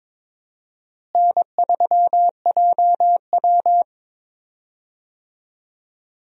Morse code, heard as a keyed tone, about 21 wpm, 700 Hz. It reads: N3JW
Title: USA Call Signs - 4 Characters - Repeated 3x - 22wpm